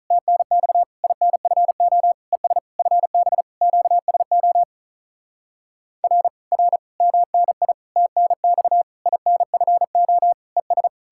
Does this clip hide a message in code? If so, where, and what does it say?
Yes: Morse code from 0.1 to 10.9 s, heard as a keyed tone, about 45 wpm, 700 Hz. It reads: TNX INFO ES FB QSO R R MNI TNX INFO ES